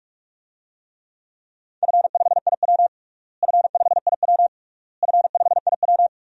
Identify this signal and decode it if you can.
Morse code, heard as a keyed tone, about 47 wpm, 690 Hz. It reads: F5IW F5IW F5IW
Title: DXCC-100 Call Signs - 4 Characters - Repeated 3x - 45wpm